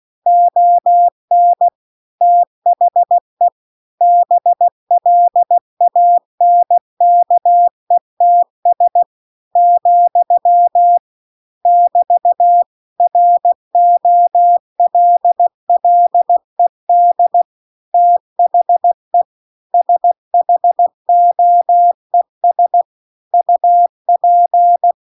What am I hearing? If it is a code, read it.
Morse code, heard as a keyed tone, about 16 wpm, 700 Hz. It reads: ON THE BLANKETS , <BT> ROLLED THE SHOES UP